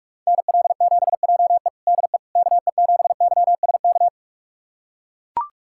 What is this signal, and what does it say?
Morse code, heard as a keyed tone, about 41 wpm, 700 Hz. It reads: NL7JE DE KE7YSK
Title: USA Callsign DE Callsign 45wpm